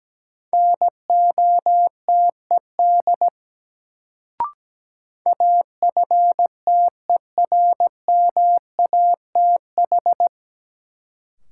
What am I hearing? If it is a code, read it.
Morse code, heard as a keyed tone, about 17 wpm, 700 Hz. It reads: NOTED AFTERMATH
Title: Sets of 5 Words - New Word Drill - 17wpm